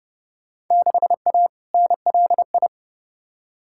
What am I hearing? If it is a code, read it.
Morse code, heard as a keyed tone, about 30 wpm, 700 Hz. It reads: 6A NLI
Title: Field Day Exchanges Encore - 30wpm